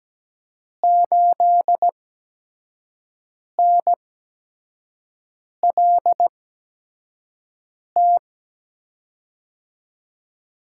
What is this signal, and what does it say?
Morse code, heard as a keyed tone, about 17 wpm, spaced as for 5 wpm, 700 Hz. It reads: 8NLT
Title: USA Call Signs - 8x Character Spacing - 17wpm